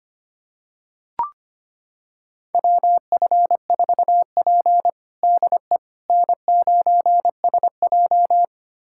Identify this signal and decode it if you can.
Morse code, heard as a keyed tone, about 25 wpm, 700 Hz. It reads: WF4P DE N9SJ
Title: USA Callsign DE Callsign - 4 Characters - 25wpm